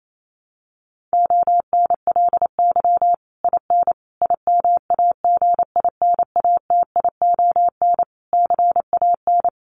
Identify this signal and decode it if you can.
Morse code, heard as a keyed tone, about 28 wpm, 700 Hz. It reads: ONLY IN IMAGINATION CAN
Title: Sentences from Top 1300 Words 28wpm